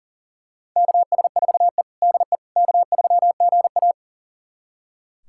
Morse code, heard as a keyed tone, about 40 wpm, 700 Hz. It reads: KS4E DE K3GA
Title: USA Callsign DE Callsign - 4 Characters - 40wpm